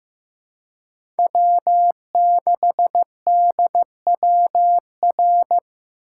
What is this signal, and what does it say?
Morse code, heard as a keyed tone, about 15 wpm, 700 Hz. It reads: W6DWR
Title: USA Call Signs - 5 Characters - 2x Character Spacing - 15wpm